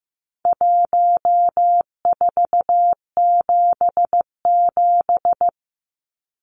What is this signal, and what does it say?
Morse code, heard as a keyed tone, about 15 wpm, 700 Hz. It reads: 1477
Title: Numbers - Contest Style - 4-Digit 15wpm